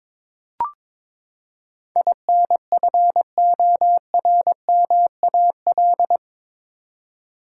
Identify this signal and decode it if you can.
Morse code, heard as a keyed tone, about 22 wpm, 700 Hz. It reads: INFORMAL
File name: practice / 22wpm / Eight Letter Words - 22wpm